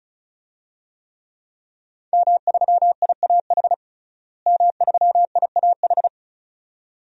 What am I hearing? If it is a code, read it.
Morse code, heard as a keyed tone, about 34 wpm, 700 Hz. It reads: M3IAH M3IAH
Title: DXCC-100 Call Signs - 5 Characters - Repeated 2x - 35wpm